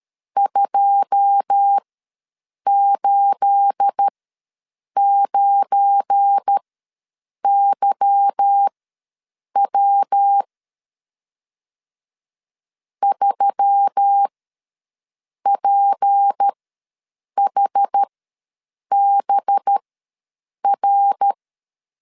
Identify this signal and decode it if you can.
Morse code, heard as a keyed tone, about 13 wpm, spaced as for 7 wpm, 800 Hz. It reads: 289YW 3PHBR
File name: LEZIONI / 9 28 ws 12wpm 3354